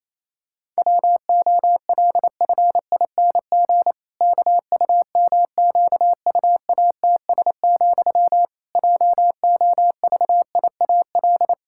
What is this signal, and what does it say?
Morse code, heard as a keyed tone, about 28 wpm, 700 Hz. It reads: WOLFING KUMQUATS, JOVIAL
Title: Pangram Challenge v2 - 28wpm